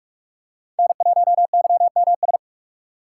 Morse code, heard as a keyed tone, about 47 wpm, 700 Hz. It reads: N1YMS
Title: USA Call Signs - 5 Characters - 4x Character Spacing - 45wpm